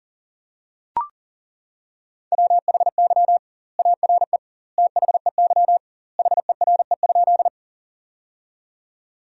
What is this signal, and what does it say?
Morse code, heard as a keyed tone, about 37 wpm, 700 Hz. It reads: WHY ARE THEY HERE?
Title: 2x Word-Spacing - Sentences from Top 200 Words - 40wpm